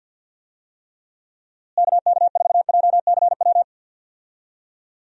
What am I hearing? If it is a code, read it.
Morse code, heard as a keyed tone, about 51 wpm, 700 Hz. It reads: KK4JCW
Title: USA Call Signs - 6 Characters - Repeated 2x - 50wpm